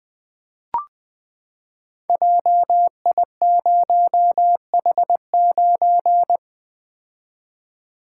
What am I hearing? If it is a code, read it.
Morse code, heard as a keyed tone, about 20 wpm, 700 Hz. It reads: JI0H9
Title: Code Groups - 5 Letter:Numbers - 20wpm